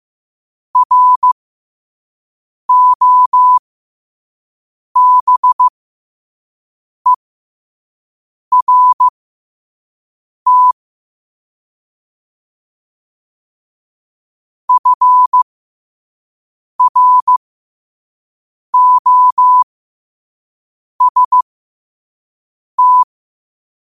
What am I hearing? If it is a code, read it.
Morse code, heard as a keyed tone, about 15 wpm, spaced as for 5 wpm, 1000 Hz. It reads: ROBERT FROST
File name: qotdfiles / 5wpm